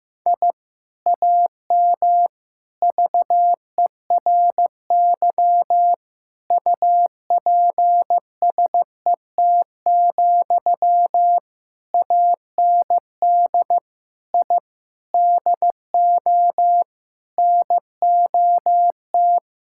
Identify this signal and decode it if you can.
Morse code, heard as a keyed tone, about 15 wpm, 700 Hz. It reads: I AM VERY UPSET, AND I DO NOT
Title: Sentences from Top 1000 Words - Review Part 5 - 15wpm